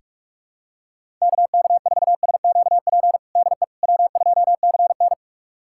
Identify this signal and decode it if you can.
Morse code, heard as a keyed tone, about 46 wpm, 700 Hz. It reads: KK4SQP DE W2CN
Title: USA Callsign DE Callsign 45wpm